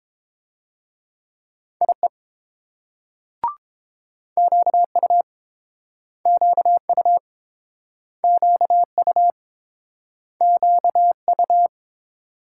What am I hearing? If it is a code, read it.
Morse code, heard as a keyed tone, about 30 wpm, 700 Hz. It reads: IE QU QU QU QU